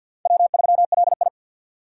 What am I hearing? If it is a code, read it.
Morse code, heard as a keyed tone, about 47 wpm, 700 Hz. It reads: W3LI
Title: USA Call Signs - 3x Character Spacing - 50wpm